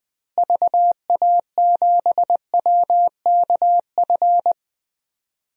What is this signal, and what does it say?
Morse code, heard as a keyed tone, about 21 wpm, 700 Hz. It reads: VA7WKF
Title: DXCC-100 Call Signs - 6 Characters - Repeated 3x - 20wpm